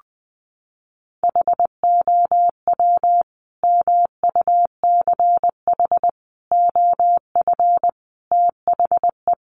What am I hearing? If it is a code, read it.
Morse code, heard as a keyed tone, about 20 wpm, 700 Hz. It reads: HOW MUCH OF THE
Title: Sentences from Top 1200 Words 20wpm